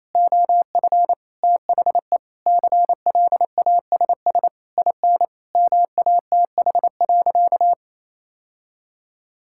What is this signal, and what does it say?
Morse code, heard as a keyed tone, about 28 wpm, 700 Hz. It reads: OF THE CLASS IN MATH.
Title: Sentences from Top 1000 Words - Review Part 6 - 28wpm